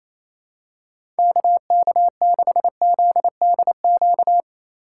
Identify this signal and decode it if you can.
Morse code, heard as a keyed tone, about 28 wpm, 700 Hz. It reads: KK6ZDQ
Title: USA Call Signs - 6 Characters - 2x Character Spacing - 28wpm